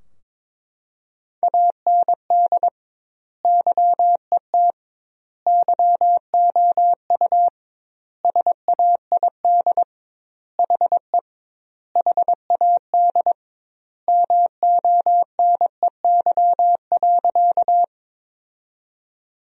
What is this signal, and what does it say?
Morse code, heard as a keyed tone, about 22 wpm, 700 Hz. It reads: AND YET YOU SAID HE HAD MONEY.
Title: 2x Word-Spacing - Sentences from Top 500 Words - 22wpm